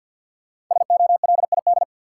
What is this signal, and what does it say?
Morse code, heard as a keyed tone, about 53 wpm, 700 Hz. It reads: SOLID
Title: Sentences from Top 1500 Words - New Word Drill - 50wpm